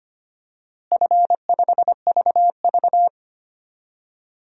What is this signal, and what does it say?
Morse code, heard as a keyed tone, about 25 wpm, 700 Hz. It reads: F5VV
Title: DXCC-100 Call Signs - 4 Characters - Repeated 3x - 25wpm